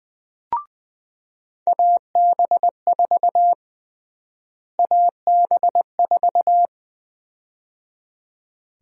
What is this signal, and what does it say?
Morse code, heard as a keyed tone, about 20 wpm, 700 Hz. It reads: AB4 AB4
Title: USA Call Sign Prefixes - Repeated 2x - 20wpm